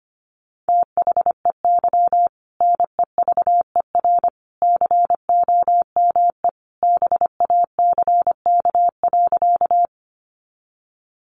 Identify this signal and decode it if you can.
Morse code, heard as a keyed tone, about 25 wpm, 700 Hz. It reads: THEY NEVER COME BACK.
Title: Sentences from Top 300 Words 25wpm